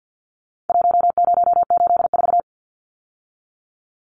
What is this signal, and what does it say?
Morse code, heard as a keyed tone, about 50 wpm, 700 Hz. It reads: 1084
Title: Numbers - Contest Style - 4-Digit 50wpm